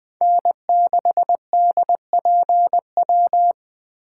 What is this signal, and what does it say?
Morse code, heard as a keyed tone, about 20 wpm, 700 Hz. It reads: N6DPW
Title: Speed Racing - USA 1x3 Call Signs 17wpm